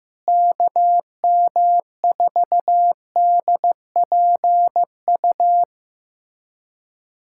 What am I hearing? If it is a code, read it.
Morse code, heard as a keyed tone, about 16 wpm, 700 Hz. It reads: KM4DPU
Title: USA Callsign DE Callsign 15wpm